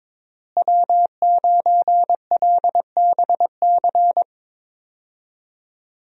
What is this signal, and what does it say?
Morse code, heard as a keyed tone, about 21 wpm, 700 Hz. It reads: W9LBC
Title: USA Call Signs - 5 Characters - Repeated 3x - 22wpm